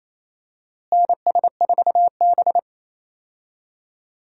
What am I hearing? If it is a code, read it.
Morse code, heard as a keyed tone, about 27 wpm, 700 Hz. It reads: NS4B
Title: USA Call Signs - 4 Characters - Repeated 3x - 28wpm